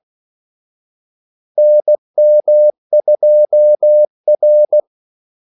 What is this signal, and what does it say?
Morse code, heard as a keyed tone, about 16 wpm, 600 Hz. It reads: NM2R
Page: Kaldesignaler 16 wpm